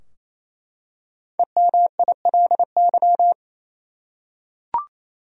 Morse code, heard as a keyed tone, about 28 wpm, 700 Hz. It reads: EMILY
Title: USA Common Names - 28wpm